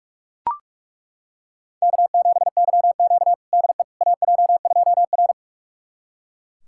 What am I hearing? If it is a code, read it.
Morse code, heard as a keyed tone, about 41 wpm, 700 Hz. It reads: K7YQ DE AJ2R
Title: USA Callsign DE Callsign - 4 Characters - 45wpm